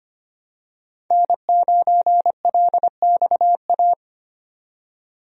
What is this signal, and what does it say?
Morse code, heard as a keyed tone, about 25 wpm, 700 Hz. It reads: N9LXA
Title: USA Call Signs - 5 Characters - 25wpm